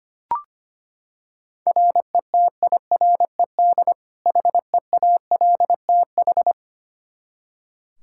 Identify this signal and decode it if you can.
Morse code, heard as a keyed tone, about 25 wpm, 700 Hz. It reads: RETIRED HEALTH
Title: Practice QSO - Exchange 3 - More Info - 25wpm